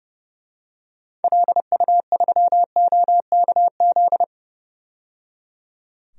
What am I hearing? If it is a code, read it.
Morse code, heard as a keyed tone, about 30 wpm, 700 Hz. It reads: LU3OKZ
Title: DXCC-100 Call Signs - 6 Characters - 30wpm